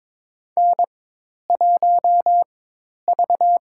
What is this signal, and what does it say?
Morse code, heard as a keyed tone, about 22 wpm, spaced as for 10 wpm, 700 Hz. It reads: N1V
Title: USA Call Signs - 4x Character Spacing - 22wpm